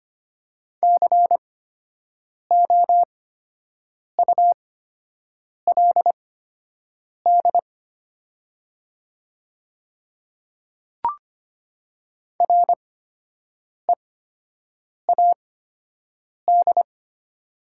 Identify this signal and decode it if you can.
Morse code, heard as a keyed tone, about 25 wpm, spaced as for 7 wpm, 700 Hz. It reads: COULD READ
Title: 8x Character Spacing - Top 200 Words - 25wpm